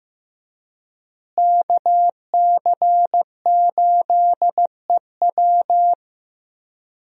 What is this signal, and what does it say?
Morse code, heard as a keyed tone, about 15 wpm, 700 Hz. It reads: KC8EW
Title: USA Call Signs - 5 Characters - Repeated 3x - 15wpm